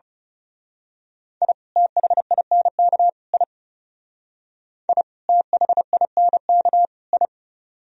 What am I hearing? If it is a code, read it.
Morse code, heard as a keyed tone, about 34 wpm, 700 Hz. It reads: I THINK I I THINK I
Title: Speed-Racing - Sets of 3 Words 30wpm